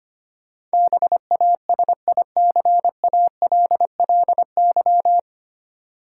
Morse code, heard as a keyed tone, about 25 wpm, 700 Hz. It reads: BASICALLY
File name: practice / 25wpm / 9 Letter Words - First 3 Letters Given - 25wpm